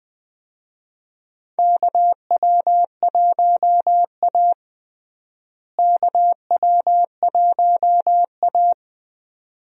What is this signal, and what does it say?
Morse code, heard as a keyed tone, about 20 wpm, 700 Hz. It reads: KW1A KW1A
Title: USA Call Signs - 4 Characters - Repeated 2x - 20wpm